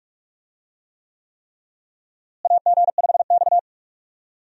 Morse code, heard as a keyed tone, about 42 wpm, 700 Hz. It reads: AG5X
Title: USA Call Signs - 4 Characters - Repeated 3x - 45wpm